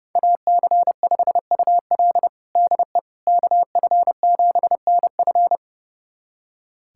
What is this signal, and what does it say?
Morse code, heard as a keyed tone, about 32 wpm, 700 Hz. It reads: AC5UL DE KF7NF
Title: USA Callsign DE Callsign 30wpm